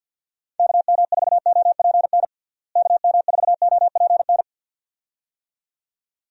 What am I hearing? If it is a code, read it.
Morse code, heard as a keyed tone, about 48 wpm, 700 Hz. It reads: KM4OPN KM4OPN
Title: USA Call Signs - 6 Characters - Repeated 2x - 50wpm